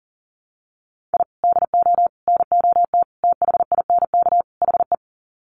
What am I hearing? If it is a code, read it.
Morse code, heard as a keyed tone, about 40 wpm, 700 Hz. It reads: I DO NOT THINK HE
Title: Sets of 5 Words 40wpm